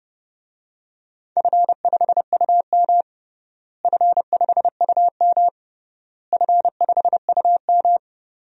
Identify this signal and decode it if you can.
Morse code, heard as a keyed tone, about 31 wpm, 700 Hz. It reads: F5UM F5UM F5UM
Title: DXCC-100 Call Signs - 4 Characters - Repeated 3x - 30wpm